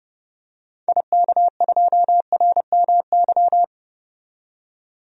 Morse code, heard as a keyed tone, about 30 wpm, 700 Hz. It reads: IK2RMY
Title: DXCC-100 Call Signs - 6 Characters - 30wpm